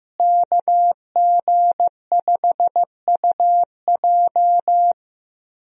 Morse code, heard as a keyed tone, about 15 wpm, 700 Hz. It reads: KG5UJ
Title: USA Call Signs - 5 Characters - Repeated 3x - 15wpm